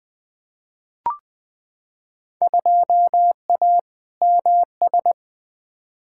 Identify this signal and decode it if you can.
Morse code, heard as a keyed tone, about 20 wpm, 700 Hz. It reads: 2A MS
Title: Field Day Exchanges - 20wpm